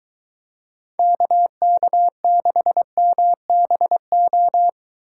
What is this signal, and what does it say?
Morse code, heard as a keyed tone, about 22 wpm, 700 Hz. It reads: KK6MBO